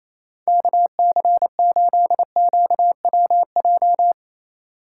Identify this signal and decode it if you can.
Morse code, heard as a keyed tone, about 34 wpm, 700 Hz. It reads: KC8QWJ
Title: Speed Racing - USA 2x3 Call Signs 28wpm